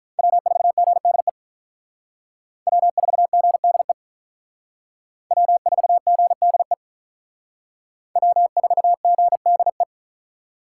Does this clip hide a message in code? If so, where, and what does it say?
Yes: Morse code from 0.2 to 9.8 s, heard as a keyed tone, about 45 wpm, 700 Hz. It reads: W4GDE W4GDE W4GDE W4GDE